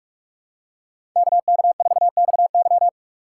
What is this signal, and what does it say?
Morse code, heard as a keyed tone, about 46 wpm, 700 Hz. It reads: KK4XY
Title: USA Call Signs - 5 Characters - 2x Character Spacing - 45wpm